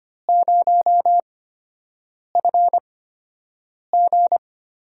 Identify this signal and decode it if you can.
Morse code, heard as a keyed tone, about 25 wpm, spaced as for 7 wpm, 700 Hz. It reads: 0FG
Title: USA Call Signs - 4 Characters - 8x Character Spacing - 25wpm